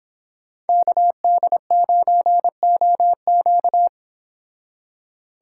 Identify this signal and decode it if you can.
Morse code, heard as a keyed tone, about 30 wpm, 700 Hz. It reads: KD9OQ